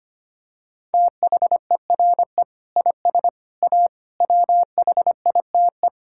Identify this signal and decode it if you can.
Morse code, heard as a keyed tone, about 25 wpm, 700 Hz. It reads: THERE IS A WHITE
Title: Sentences from Top 1600 Words 25wpm